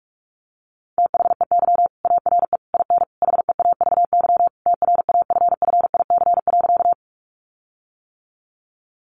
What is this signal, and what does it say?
Morse code, heard as a keyed tone, about 44 wpm, 700 Hz. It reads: THEY ARE IN HEAVY TRAFFIC.
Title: Sentences from Top 1300 Words 45wpm